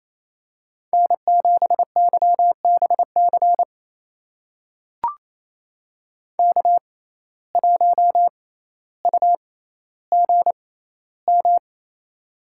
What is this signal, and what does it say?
Morse code, heard as a keyed tone, about 28 wpm, spaced as for 14 wpm, 700 Hz. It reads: N7YBC K1UGM
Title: USA Call Signs - 6x Character Spacing - 28wpm